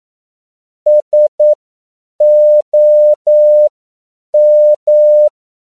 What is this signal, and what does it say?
Morse code, heard as a keyed tone, about 9 wpm, spaced as for 7 wpm, 600 Hz. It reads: SOM
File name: CW / 10PPM